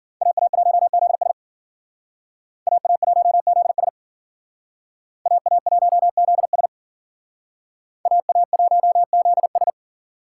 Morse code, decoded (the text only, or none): AA1ZS AA1ZS AA1ZS AA1ZS